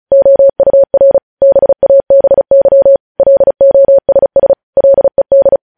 Morse code, heard as a keyed tone, about 36 wpm, 550 Hz. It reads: OUR BABY LOSS LED
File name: podcast / News headlines-1x-35x35